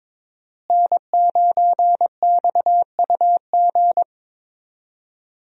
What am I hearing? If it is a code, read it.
Morse code, heard as a keyed tone, about 22 wpm, 700 Hz. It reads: N9XUG
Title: USA Call Signs - 5 Characters - Repeated 3x - 22wpm